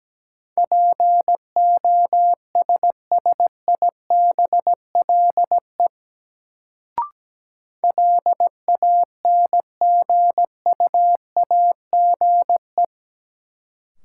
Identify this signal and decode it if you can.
Morse code, heard as a keyed tone, about 17 wpm, 700 Hz. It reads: POSSIBLE LANGUAGE
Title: Sentences from Top 500 Words - New Word Drill - 17wpm